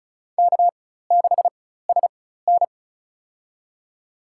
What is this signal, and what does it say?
Morse code, heard as a keyed tone, about 36 wpm, spaced as for 16 wpm, 700 Hz. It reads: K6SN
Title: USA Call Signs - 4 Characters - 4x Character Spacing - 35wpm